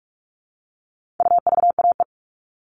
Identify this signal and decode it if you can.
Morse code, heard as a keyed tone, about 44 wpm, 700 Hz. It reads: UVAE
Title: All Cut Numbers - 4-Digit 45wpm